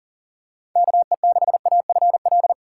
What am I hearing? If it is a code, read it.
Morse code, heard as a keyed tone, about 40 wpm, 700 Hz. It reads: KE6AFL
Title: USA Call Signs - 6 Characters - 6x Character Spacing - 40wpm